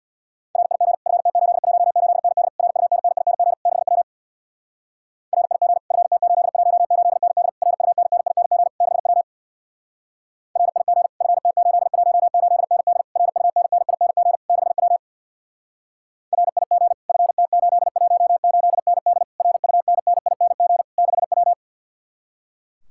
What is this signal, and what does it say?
Morse code, heard as a keyed tone, about 60 wpm, 700 Hz. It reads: RIG FT818ND RUNNING 6W RIG FT818ND RUNNING 6W RIG FT818ND RUNNING 6W RIG FT818ND RUNNING 6W
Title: Speed-Racing Common QSO Phrases - 50wpm